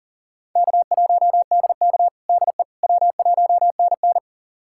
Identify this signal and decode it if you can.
Morse code, heard as a keyed tone, about 40 wpm, 700 Hz. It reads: K1DK DE W1NN
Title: USA Callsign DE Callsign - 4 Characters - 40wpm